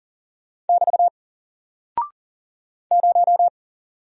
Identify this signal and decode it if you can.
Morse code, heard as a keyed tone, about 40 wpm, 700 Hz. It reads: <BT> 0
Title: Single Character (Letter:Number:Punctuation:Prosign) Alt 40wpm